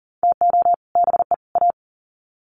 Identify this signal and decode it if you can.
Morse code, heard as a keyed tone, about 40 wpm, 700 Hz. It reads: TO BE A
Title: Sets of 3 Words 40wpm